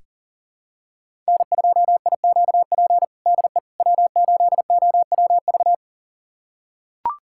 Morse code, heard as a keyed tone, about 40 wpm, 700 Hz. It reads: N2IQP DE W8OWV
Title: USA Callsign DE Callsign - 5 Characters - 40wpm